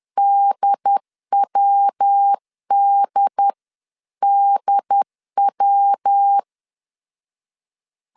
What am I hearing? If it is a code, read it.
Morse code, heard as a keyed tone, about 10 wpm, 800 Hz. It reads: DWDDW
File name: CW / DET / 1 3 1500C S C 10wpm 3335